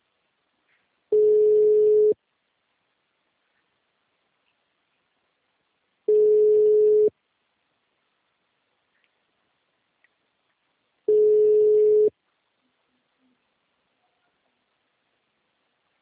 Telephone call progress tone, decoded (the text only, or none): ringback tone